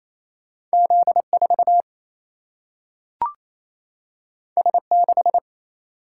Morse code, heard as a keyed tone, about 28 wpm, 700 Hz. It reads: Z4 S6
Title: Single-Letter Single-Number 28wpm